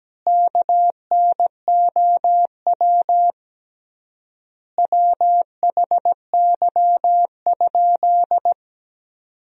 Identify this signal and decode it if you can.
Morse code, heard as a keyed tone, about 17 wpm, 700 Hz. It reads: KNOW WHY?
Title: 3x Word-Spacing - Sentences from Top 200 Words - 17wpm